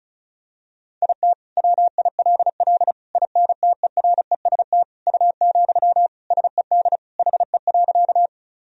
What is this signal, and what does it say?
Morse code, heard as a keyed tone, about 36 wpm, 700 Hz. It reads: IT WILL INTEREST U, SED HE.
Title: Sentences from Top 400 Words and Abbreviations 35wpm